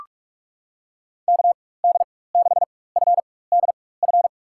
Morse code, heard as a keyed tone, about 46 wpm, spaced as for 21 wpm, 700 Hz. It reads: KD6FDF
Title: USA Call Signs - 4x Character Spacing - 45wpm